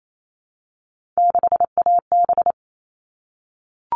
Morse code, heard as a keyed tone, about 28 wpm, 700 Hz. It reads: 6AB
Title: All Cut Numbers - 3-Digit 28wpm